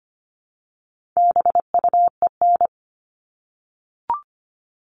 Morse code, heard as a keyed tone, about 25 wpm, 700 Hz. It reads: BUEN
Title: All Cut Numbers - 4-Digit 25wpm